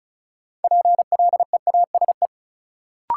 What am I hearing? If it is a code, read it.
Morse code, heard as a keyed tone, about 36 wpm, 700 Hz. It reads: PLEASE
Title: Sentences from Top 1500 Words - New Word Drill - 35wpm